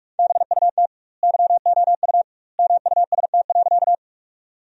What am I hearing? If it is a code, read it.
Morse code, heard as a keyed tone, about 43 wpm, 700 Hz. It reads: BUT YOU MUST.
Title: 2x Word-Spacing - Sentences from Top 200 Words - 45wpm